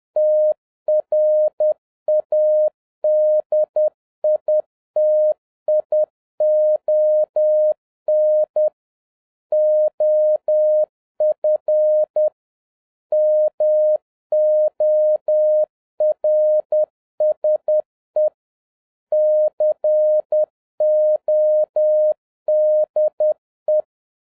Morse code, heard as a keyed tone, about 10 wpm, 610 Hz. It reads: TRADITION OF MORSE CODE